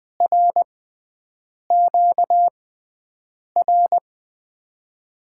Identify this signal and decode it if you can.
Morse code, heard as a keyed tone, about 20 wpm, 700 Hz. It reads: R Q R
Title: USA Call Signs - 6x Character Spacing - 20wpm